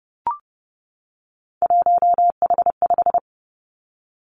Morse code, heard as a keyed tone, about 30 wpm, 700 Hz. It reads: 1H5
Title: Three Challenging Letter-Number Combinations 30wpm